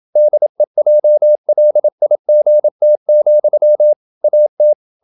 Morse code, heard as a keyed tone, about 27 wpm, 600 Hz. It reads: DEJLIGT, AT
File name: Ferie QRQ 27wpm